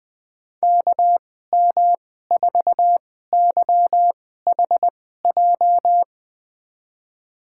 Morse code, heard as a keyed tone, about 20 wpm, spaced as for 14 wpm, 700 Hz. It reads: KM4YHJ
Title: USA Call Signs - 2x Character Spacing - 20wpm